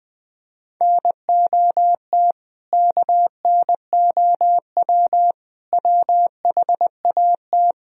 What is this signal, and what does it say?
Morse code, heard as a keyed tone, about 20 wpm, 700 Hz. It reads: NOT KNOW WHAT
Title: Speed-Racing - Sets of 5 Words 20wpm